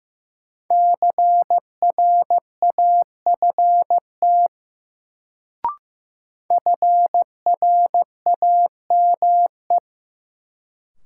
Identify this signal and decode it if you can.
Morse code, heard as a keyed tone, about 15 wpm, 700 Hz. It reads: CRAFT FRAME
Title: Sentences from Top 1000 Words - Review Part 6 - New Word Drill - 15wpm